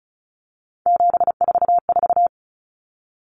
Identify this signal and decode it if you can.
Morse code, heard as a keyed tone, about 34 wpm, 700 Hz. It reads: 744
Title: Numbers - 3-Digit 35wpm